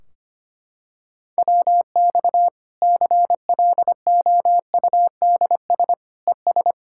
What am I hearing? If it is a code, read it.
Morse code, heard as a keyed tone, about 25 wpm, 700 Hz. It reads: WX CLOUDS ES
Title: Practice QSO - Exchange 2 - Info - 25wpm